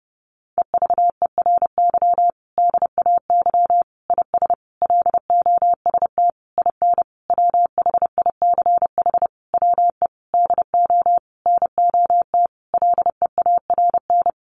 Morse code, heard as a keyed tone, about 30 wpm, 700 Hz. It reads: EVERY DAY IS LOST IN WHICH WE DO NOT LEARN
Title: Sentences from Top 1400 Words 30wpm